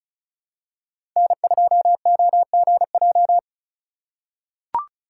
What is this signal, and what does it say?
Morse code, heard as a keyed tone, about 36 wpm, 700 Hz. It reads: N2OGJ
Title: USA Call Signs - 5 Characters - 6x Character Spacing - 35wpm